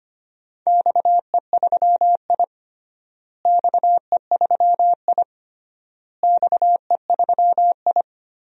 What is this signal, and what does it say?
Morse code, heard as a keyed tone, about 25 wpm, 700 Hz. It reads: XE3I XE3I XE3I
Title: DXCC-100 Call Signs - 4 Characters - Repeated 3x - 25wpm